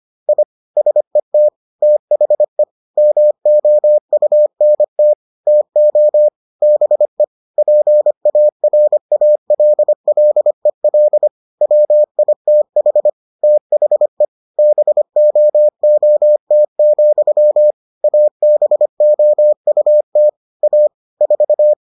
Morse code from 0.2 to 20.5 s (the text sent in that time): I SET THE MOUNT TO BE PARALLEL WITH THE BOOT, ABOUT